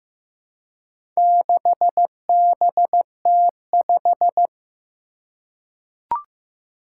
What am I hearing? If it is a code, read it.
Morse code, heard as a keyed tone, about 15 wpm, 700 Hz. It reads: 6BT5
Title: Four Challenging Letter-Number Combinations 15wpm